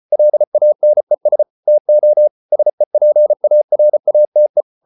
Morse code, heard as a keyed tone, about 34 wpm, 600 Hz. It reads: LANES TO SEPARATE